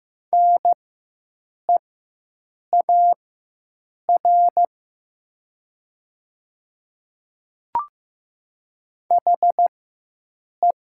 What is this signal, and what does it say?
Morse code, heard as a keyed tone, about 15 wpm, spaced as for 7 wpm, 700 Hz. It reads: NEAR HE
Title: 4x Character Spacing - Top 200 Words - 15wpm